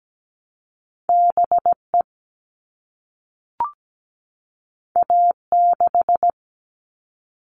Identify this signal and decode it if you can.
Morse code, heard as a keyed tone, about 17 wpm, 700 Hz. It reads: BE A6
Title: All Cut Numbers - 2-Digit 17wpm